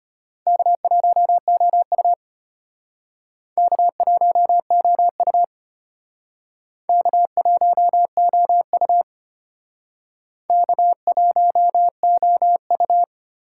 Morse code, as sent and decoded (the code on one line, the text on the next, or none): -.- .---- --- ..- / -.- .---- --- ..- / -.- .---- --- ..- / -.- .---- --- ..-
K1OU K1OU K1OU K1OU